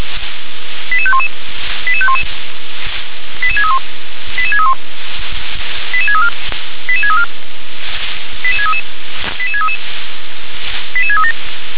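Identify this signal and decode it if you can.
Five-tone selective call (ZVEI), heard as quick runs of five tones, tones of about 70 ms. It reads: 88511 88511 88631 88631 88634 88634 88633 88633 88637
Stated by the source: ZVEI five-tone selective call received by SDR.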